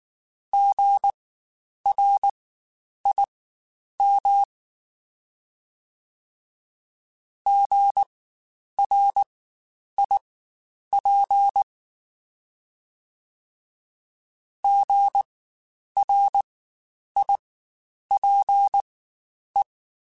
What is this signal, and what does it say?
Morse code, heard as a keyed tone, about 19 wpm, spaced as for 9 wpm, 800 Hz. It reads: GRIM GRIP GRIPE
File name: Spaced Alphas / Spaced Alphas mp3s / Gs sp @18